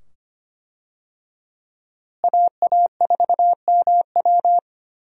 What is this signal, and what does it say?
Morse code, heard as a keyed tone, about 25 wpm, 700 Hz. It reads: AA4MW
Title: USA Call Signs - 5 Characters - Repeated 2x - 25wpm